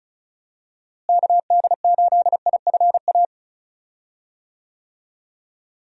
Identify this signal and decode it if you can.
Morse code, heard as a keyed tone, about 36 wpm, 700 Hz. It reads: KD8IFA
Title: USA Call Signs - 6 Characters - Repeated 3x - 35wpm